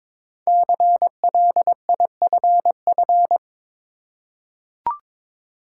Morse code, heard as a keyed tone, about 22 wpm, 700 Hz. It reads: CLIFF
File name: 5 Letter Words - First 3 Letters Given - 22wpm